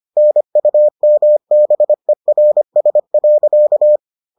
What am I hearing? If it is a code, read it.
Morse code, heard as a keyed tone, about 25 wpm, 600 Hz. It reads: NUMBERS.